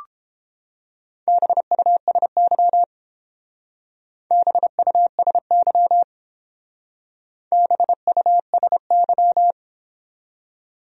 Morse code, heard as a keyed tone, about 28 wpm, 700 Hz. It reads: BUSY BUSY BUSY